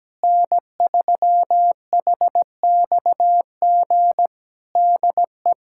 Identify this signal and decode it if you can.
Morse code, heard as a keyed tone, about 17 wpm, 700 Hz. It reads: N3HXG DE
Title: Practice QSO - Exchange 4 - Ending - 17wpm